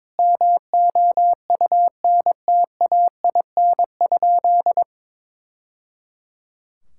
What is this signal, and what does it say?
Morse code, heard as a keyed tone, about 22 wpm, 700 Hz. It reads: MOUNTAIN?
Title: Sentences from Top 1000 Words - Review Part 5 - 22wpm